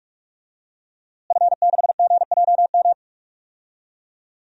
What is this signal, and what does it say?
Morse code, heard as a keyed tone, about 46 wpm, 700 Hz. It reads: F6GJM
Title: DXCC-100 Call Signs - 5 Characters - Repeated 3x - 45wpm